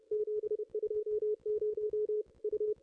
Morse code, heard as a keyed tone, about 28 wpm, 440 Hz. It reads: 720 U
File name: contests / ss-cw-k5tr-op-n5rz -0600